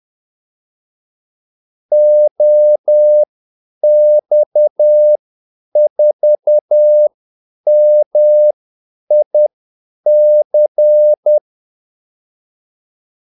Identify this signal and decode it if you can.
Morse code, heard as a keyed tone, about 10 wpm, spaced as for 8 wpm, 600 Hz. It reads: OX4MIC
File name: Callsigns-10-8wpm